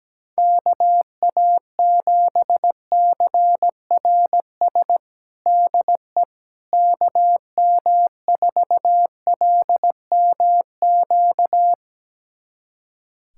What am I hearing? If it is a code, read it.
Morse code, heard as a keyed tone, about 17 wpm, 700 Hz. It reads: KA7CRS DE KM4LMQ
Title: USA Callsign DE Callsign 17wpm